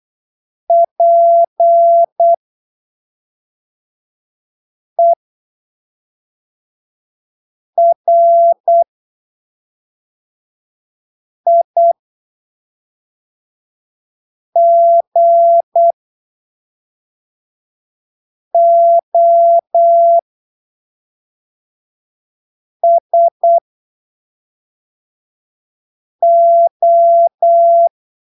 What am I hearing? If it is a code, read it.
Morse code, heard as a keyed tone, about 8 wpm, spaced as for 3 wpm, 680 Hz. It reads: PERIGOSO
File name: texto 24 08ppm